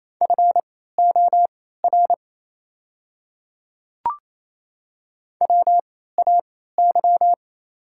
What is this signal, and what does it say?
Morse code, heard as a keyed tone, about 27 wpm, spaced as for 16 wpm, 690 Hz. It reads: FOR WAY
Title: 3x Character Spacing - Top 100 Words - 28wpm